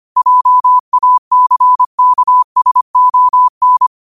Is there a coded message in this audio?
Yes: Morse code from 0.2 to 3.9 s, heard as a keyed tone, about 25 wpm, 1000 Hz. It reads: JACKSON